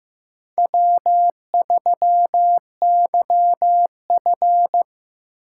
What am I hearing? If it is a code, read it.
Morse code, heard as a keyed tone, about 15 wpm, 700 Hz. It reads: W3YF
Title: USA Call Signs - 4 Characters - 6x Character Spacing - 15wpm